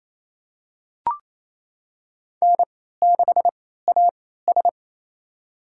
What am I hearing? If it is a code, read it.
Morse code, heard as a keyed tone, about 28 wpm, spaced as for 16 wpm, 700 Hz. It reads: N6AS
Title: USA Call Signs - 4 Characters - 3x Character Spacing - 28wpm